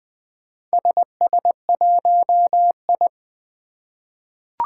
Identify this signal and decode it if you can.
Morse code, heard as a keyed tone, about 20 wpm, 700 Hz. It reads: SS1I
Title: Four Challenging Letter-Number Combinations 20wpm